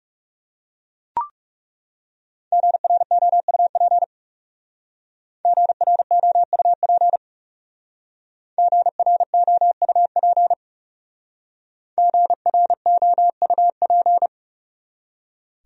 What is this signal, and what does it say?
Morse code, heard as a keyed tone, about 39 wpm, 700 Hz. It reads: GROUP GROUP GROUP GROUP